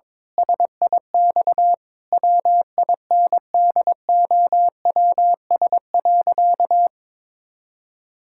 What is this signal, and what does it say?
Morse code, heard as a keyed tone, about 22 wpm, 700 Hz. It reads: SIX WINDOWS.
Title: Pangram Challenge v2 - 22wpm